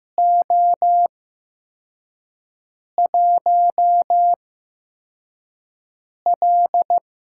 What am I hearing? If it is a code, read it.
Morse code, heard as a keyed tone, about 15 wpm, spaced as for 4 wpm, 700 Hz. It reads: O1L
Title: USA Call Signs - 4 Characters - 8x Character Spacing - 15wpm